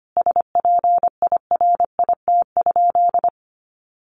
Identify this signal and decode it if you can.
Morse code, heard as a keyed tone, about 25 wpm, 700 Hz. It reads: SPIRIT?
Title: Sentences from Top 1400 Words 25wpm